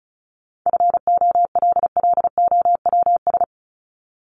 Morse code, heard as a keyed tone, about 34 wpm, 700 Hz. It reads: FOLLOWS
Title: ES:S Suffix - Top 100 Words with Base Words 35wpm